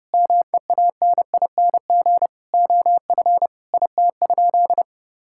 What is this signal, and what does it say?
Morse code, heard as a keyed tone, about 30 wpm, 700 Hz. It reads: MEANING OF IT?
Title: Sentences from Top 1000 Words - Review Part 6 - 30wpm